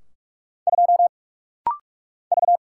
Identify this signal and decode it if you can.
Morse code, heard as a keyed tone, about 45 wpm, 700 Hz. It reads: 2 V
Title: Single Letter-Number Alt - Rapid-Fire 45wpm